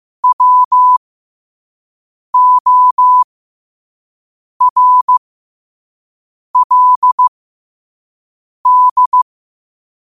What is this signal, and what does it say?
Morse code, heard as a keyed tone, about 15 wpm, spaced as for 5 wpm, 1000 Hz. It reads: WORLD
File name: qotdfiles / 5wpm